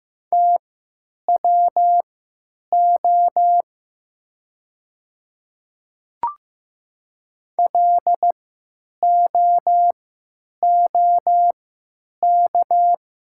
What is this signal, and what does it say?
Morse code, heard as a keyed tone, about 15 wpm, spaced as for 9 wpm, 700 Hz. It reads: TWO LOOK
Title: 3x Character Spacing - Top 200 Words - 15wpm